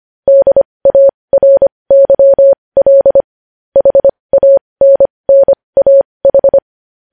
Morse code, heard as a keyed tone, about 25 wpm, spaced as for 20 wpm, 550 Hz. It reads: DARYL HANNAH
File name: podcast / News headlines-1x-25x20